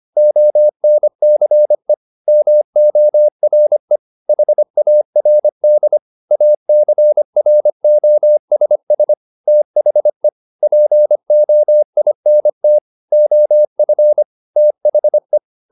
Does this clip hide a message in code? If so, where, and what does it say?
Yes: Morse code from 0.2 to 15.4 s, heard as a keyed tone, about 25 wpm, 600 Hz. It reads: ONCE MORE HARD ACROSS THE POINT OF THE